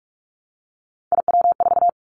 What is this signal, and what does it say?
Morse code, heard as a keyed tone, about 46 wpm, 690 Hz. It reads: IW4
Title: Callsign Prefixes for Top 50 DXCC - 45wpm